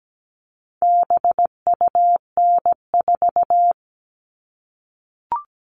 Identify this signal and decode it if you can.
Morse code, heard as a keyed tone, about 17 wpm, 700 Hz. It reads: BUN4
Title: All Cut Numbers - 4-Digit 17wpm